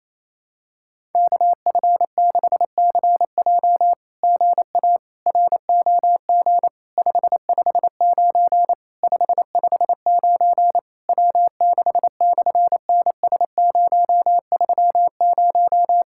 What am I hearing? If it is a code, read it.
Morse code, heard as a keyed tone, about 27 wpm, 700 Hz. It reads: KF6CJ GA ROG 559 559 W6/NS030
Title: SOTA QSO Practice 28wpm